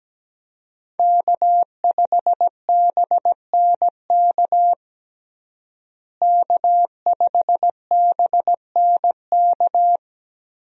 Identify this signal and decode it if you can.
Morse code, heard as a keyed tone, about 17 wpm, 700 Hz. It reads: K5BNK K5BNK
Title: USA Call Signs - 5 Characters - Repeated 3x - 17wpm